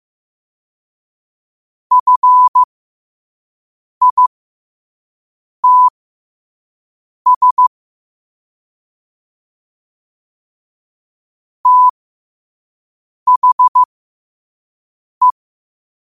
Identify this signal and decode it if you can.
Morse code, heard as a keyed tone, about 15 wpm, spaced as for 5 wpm, 1000 Hz. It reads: FITS THE
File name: qotdfiles / 5wpm